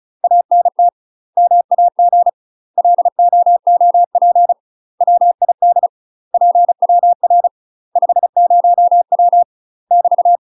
Morse code, heard as a keyed tone, about 36 wpm, 700 Hz. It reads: ANT MAG LOOP WID PWR 50W <BT>
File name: UK MASTER QSOs TRAINING / 35WPM / UK MASTER QSOs 0028